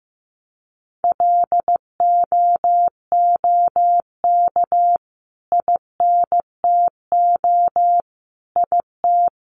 Morse code, heard as a keyed tone, about 15 wpm, 700 Hz. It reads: LOOK INTO IT
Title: Sentences from Top 1300 Words 15wpm